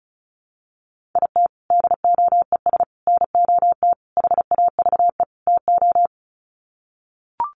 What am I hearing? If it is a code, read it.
Morse code, heard as a keyed tone, about 34 wpm, 700 Hz. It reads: IT DOES NOT HAVE TO
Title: Sets of 5 Words 35wpm